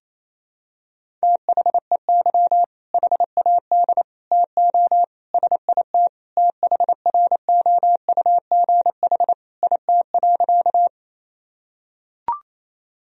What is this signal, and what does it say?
Morse code, heard as a keyed tone, about 28 wpm, 700 Hz. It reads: THEY HAD TO SIT THROUGH IT.
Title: Sentences from Top 1000 Words - Review Part 1 - 28wpm